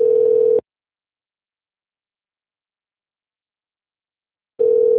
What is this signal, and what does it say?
Ringback tone (North America), 440 + 480 Hz.